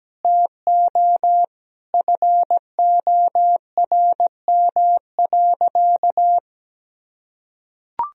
Morse code, heard as a keyed tone, about 17 wpm, 700 Hz. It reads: TO FORM.
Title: Sentences from Top 1000 Words - Review Part 1 - 17wpm